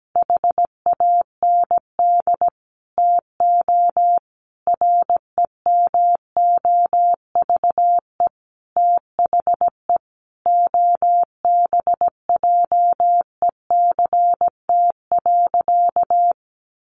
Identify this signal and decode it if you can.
Morse code, heard as a keyed tone, about 17 wpm, 700 Hz. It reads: HAND TO REMOVE THE OBJECT.
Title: Sentences from Top 800 Words 17wpm